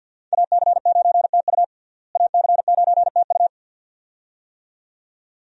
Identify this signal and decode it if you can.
Morse code, heard as a keyed tone, about 53 wpm, 700 Hz. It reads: AC9TU AC9TU
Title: USA Call Signs - 5 Characters - Repeated 2x - 50wpm